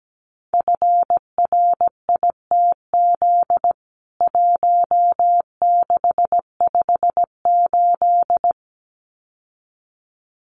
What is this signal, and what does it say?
Morse code, heard as a keyed tone, about 17 wpm, 700 Hz. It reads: FRITZ 1658
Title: CWT Exchange 17wpm